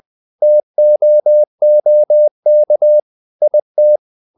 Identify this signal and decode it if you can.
Morse code, heard as a keyed tone, about 20 wpm, 600 Hz. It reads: TOOK IT
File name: the old man and the sea part 2 20wpm